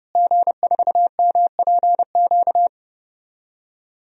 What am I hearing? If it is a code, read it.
Morse code, heard as a keyed tone, about 30 wpm, 700 Hz. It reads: G4MPQ
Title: DXCC-100 Call Signs - 5 Characters - Repeated 3x - 30wpm